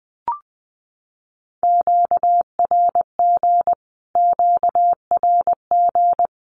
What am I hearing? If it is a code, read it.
Morse code, heard as a keyed tone, about 19 wpm, 700 Hz. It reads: QRG QRG
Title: Q-Codes Repeated 3x 20wpm